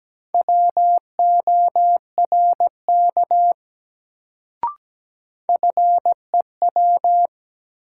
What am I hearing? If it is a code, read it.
Morse code, heard as a keyed tone, about 17 wpm, 700 Hz. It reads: WORK FEW
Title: Sets of 2 Words - New Word Drill - 17wpm